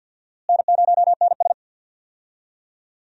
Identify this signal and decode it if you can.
Morse code, heard as a keyed tone, about 49 wpm, 700 Hz. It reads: N0NS
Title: USA Call Signs - 4 Characters - Repeated 2x - 50wpm